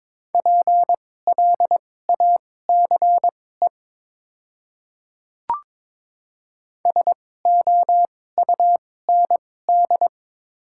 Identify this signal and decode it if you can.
Morse code, heard as a keyed tone, about 22 wpm, spaced as for 16 wpm, 700 Hz. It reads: PLACE SOUND
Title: 2x Character Spacing - Top 200 Words - 22wpm